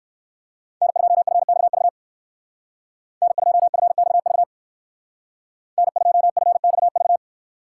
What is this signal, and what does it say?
Morse code, heard as a keyed tone, about 54 wpm, 700 Hz. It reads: N2FXV N2FXV N2FXV
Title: Speed Racing - USA 1x3 Call Signs 45wpm